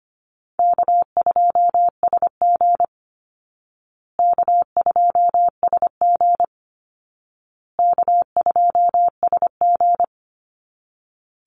Morse code, heard as a keyed tone, about 25 wpm, 700 Hz. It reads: K2SG K2SG K2SG
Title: Call Signs Repeated 2x - 25wpm